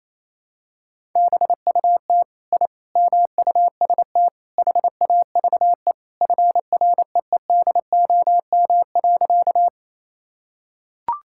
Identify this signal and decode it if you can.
Morse code, heard as a keyed tone, about 28 wpm, 700 Hz. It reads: BUT I MUST HAVE FREEDOM.
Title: Sentences from Top 1000 Words - Review Part 5 - 28wpm